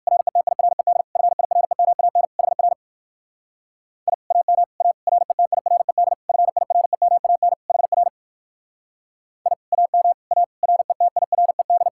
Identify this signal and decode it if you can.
Morse code, heard as a keyed tone, about 55 wpm, 700 Hz. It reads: RETIRED FIREMAN HR I AM A RETIRED FIREMAN HR I AM A RETIRED
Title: Speed-Racing Common QSO Phrases Encore - 45wpm